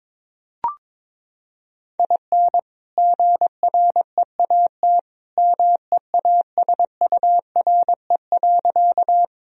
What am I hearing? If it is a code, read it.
Morse code, heard as a keyed tone, about 22 wpm, 700 Hz. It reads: IN GREAT MEASURE.
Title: Sentences from Top 1000 Words - Review Part 2 - 22wpm